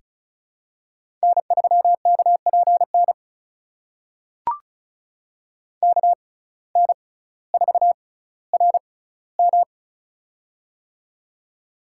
Morse code, heard as a keyed tone, about 36 wpm, spaced as for 18 wpm, 700 Hz. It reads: N3KPN KN4RM
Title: USA Call Signs - 5 Characters - 6x Character Spacing - 35wpm